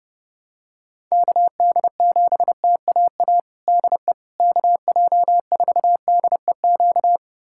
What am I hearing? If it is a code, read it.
Morse code, heard as a keyed tone, about 30 wpm, 700 Hz. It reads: KD7TAA DE KJ4DEQ
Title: USA Callsign DE Callsign 30wpm